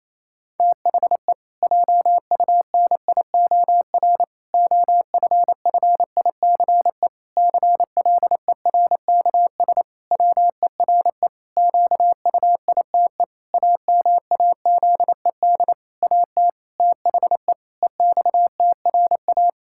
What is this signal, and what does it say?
Morse code, heard as a keyed tone, about 28 wpm, 700 Hz. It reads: THE JUNIOR OFFICE CLERKS WERE QUITE AMAZED AT THE EXTRA
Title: Pangram Challenge v2 - 28wpm